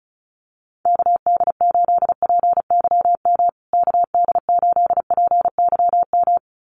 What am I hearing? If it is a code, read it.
Morse code, heard as a keyed tone, about 34 wpm, 700 Hz. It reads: KD8PYM KD8PYM
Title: USA Call Signs Repeated 2x 35wpm